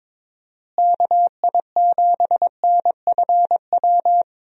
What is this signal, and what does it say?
Morse code, heard as a keyed tone, about 21 wpm, 700 Hz. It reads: KI7NFW
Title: USA Call Signs - 6 Characters - Repeated 2x - 22wpm